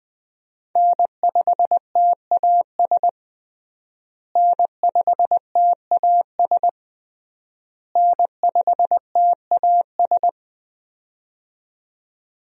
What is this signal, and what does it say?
Morse code, heard as a keyed tone, about 20 wpm, 700 Hz. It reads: N5TAS N5TAS N5TAS
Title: USA Call Signs - 5 Characters - Repeated 3x - 20wpm